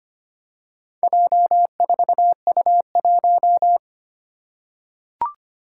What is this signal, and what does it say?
Morse code, heard as a keyed tone, about 25 wpm, 700 Hz. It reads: J4U1
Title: Four Challenging Letter-Number Combinations 25wpm